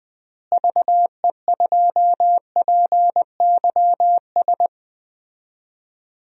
Morse code, heard as a keyed tone, about 20 wpm, 700 Hz. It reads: VE2PYS
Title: DXCC-100 Call Signs - 6 Characters - Repeated 3x - 20wpm